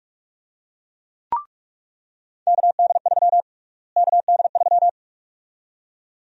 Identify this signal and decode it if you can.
Morse code, heard as a keyed tone, about 43 wpm, 700 Hz. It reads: KD3 KD3
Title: USA Call Sign Prefixes - Repeated 2x - 45wpm